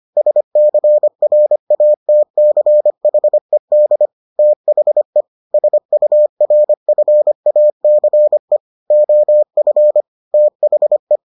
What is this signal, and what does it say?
Morse code, heard as a keyed tone, about 25 wpm, 600 Hz. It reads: SCRATCHED THE SURFACE OF THE